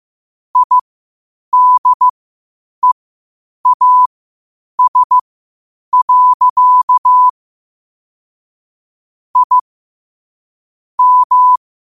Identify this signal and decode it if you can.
Morse code, heard as a keyed tone, about 15 wpm, spaced as for 8 wpm, 1000 Hz. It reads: IDEAS. I M
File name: qotdfiles / 8wpm